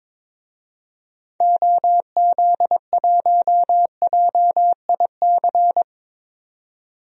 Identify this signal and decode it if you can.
Morse code, heard as a keyed tone, about 21 wpm, 700 Hz. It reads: OZ1JIC
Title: DXCC-100 Call Signs - 6 Characters - 22wpm